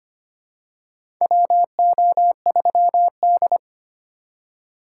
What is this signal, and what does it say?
Morse code, heard as a keyed tone, about 25 wpm, 700 Hz. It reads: WO3D
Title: USA Call Signs - 2x Character Spacing - 25wpm